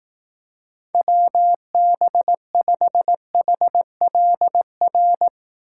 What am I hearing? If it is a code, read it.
Morse code, heard as a keyed tone, about 18 wpm, 700 Hz. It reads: WB5HLR
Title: Speed Racing - USA 2x3 Call Signs 15wpm